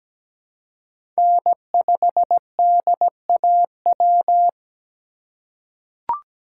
Morse code, heard as a keyed tone, about 17 wpm, 700 Hz. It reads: N5DAW
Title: USA Call Signs - 5 Characters - 4x Character Spacing - 17wpm